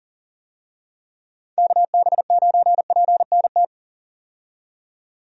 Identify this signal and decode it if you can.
Morse code, heard as a keyed tone, about 40 wpm, 700 Hz. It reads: KB9PNT
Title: USA Call Signs - 6 Characters - 40wpm